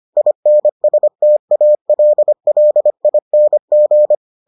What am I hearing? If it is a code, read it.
Morse code, heard as a keyed tone, about 25 wpm, 600 Hz. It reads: INSTALLING